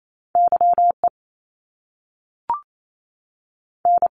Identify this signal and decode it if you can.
Morse code, heard as a keyed tone, about 28 wpm, spaced as for 6 wpm, 700 Hz. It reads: YEN
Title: Top Two Letter Combinations in Words 28wpm